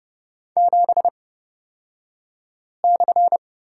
Morse code, heard as a keyed tone, about 30 wpm, spaced as for 5 wpm, 700 Hz. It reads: 7/
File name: Lesson 37d - Letter x - Two Characters 30wpm